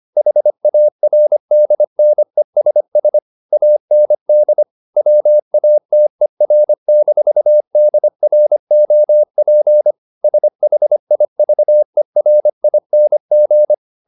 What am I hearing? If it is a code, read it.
Morse code, heard as a keyed tone, about 25 wpm, 600 Hz. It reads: HARDNESS AND WATER-DROP SHIVERING